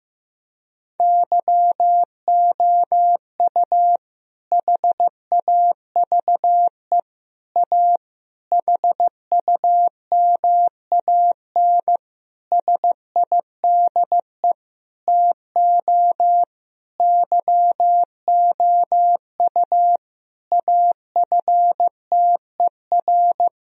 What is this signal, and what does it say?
Morse code, heard as a keyed tone, about 15 wpm, 700 Hz. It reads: YOU HAVE A HUMAN SIDE TO YOU AFTER
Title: Sentences from Top 1000 Words - Review Part 3 - 15wpm